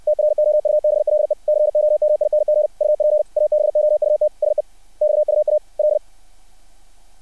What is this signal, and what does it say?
Morse code, heard as a keyed tone, about 60 wpm, 600 Hz. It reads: TROUBLE COMING UP ABOUT SEXUAL
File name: archived / CRSnet